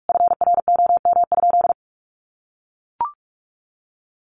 Morse code, heard as a keyed tone, about 44 wpm, 700 Hz. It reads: FROM?
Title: Sentences from Top 200 Words 45wpm